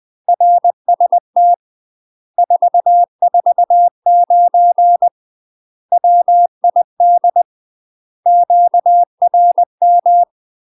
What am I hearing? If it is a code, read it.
Morse code, heard as a keyed tone, about 20 wpm, 700 Hz. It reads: RST 449 WID QRM
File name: UK MASTER QSOs TRAINING / 20WPM / UK MASTER QSOs 0061